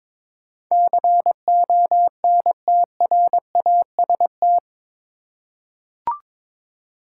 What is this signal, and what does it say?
Morse code, heard as a keyed tone, about 22 wpm, 700 Hz. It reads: CONTRAST
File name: practice / 22wpm / Eight Letter Words - 22wpm